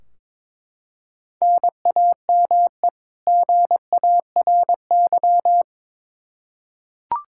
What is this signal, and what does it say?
Morse code, heard as a keyed tone, about 22 wpm, 700 Hz. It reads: NAME GARY
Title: Practice QSO - Exchange 1 - Beginning - 22wpm